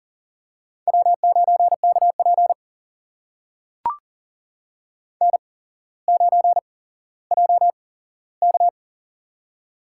Morse code, heard as a keyed tone, about 40 wpm, spaced as for 17 wpm, 700 Hz. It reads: W9KP N9JK
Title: USA Call Signs - 4 Characters - 8x Character Spacing - 40wpm